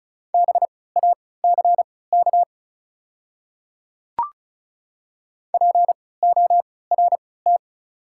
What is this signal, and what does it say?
Morse code, heard as a keyed tone, about 34 wpm, spaced as for 20 wpm, 690 Hz. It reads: BACK PORT
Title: 3x Character Spacing - Top 200 Words - 35wpm